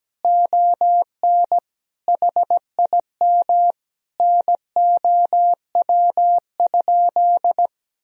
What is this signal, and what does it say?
Morse code, heard as a keyed tone, about 17 wpm, 700 Hz. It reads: ON HIM NOW?
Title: Sentences from Top 1000 Words - Review Part 2 - 17wpm